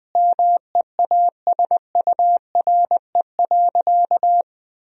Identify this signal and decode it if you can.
Morse code, heard as a keyed tone, about 20 wpm, 700 Hz. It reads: MEASURE.
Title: Sentences from Top 1000 Words - Review Part 2 - 20wpm